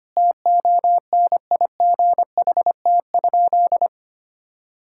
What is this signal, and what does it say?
Morse code, heard as a keyed tone, about 25 wpm, 700 Hz. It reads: TONIGHT?
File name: practice / 25wpm / Sentences with 4-Word Phrases - 25wpm